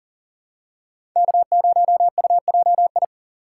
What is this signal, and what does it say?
Morse code, heard as a keyed tone, about 40 wpm, 700 Hz. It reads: K0UJI
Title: USA Call Signs - 5 Characters - 4x Character Spacing - 40wpm